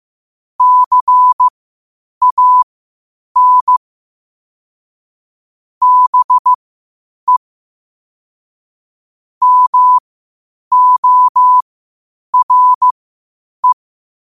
Morse code, heard as a keyed tone, about 15 wpm, spaced as for 8 wpm, 1000 Hz. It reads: CAN BE MORE
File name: qotdfiles / 8wpm